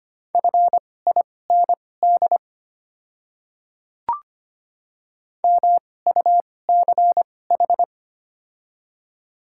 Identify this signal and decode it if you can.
Morse code, heard as a keyed tone, about 25 wpm, spaced as for 18 wpm, 690 Hz. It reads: FIND MUCH
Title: 2x Character Spacing - Top 200 Words - 25wpm